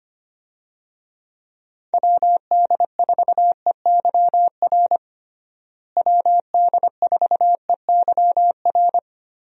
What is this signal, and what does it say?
Morse code, heard as a keyed tone, about 25 wpm, 700 Hz. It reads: WD4EYR WD4EYR
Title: USA Call Signs - 6 Characters - Repeated 3x - 25wpm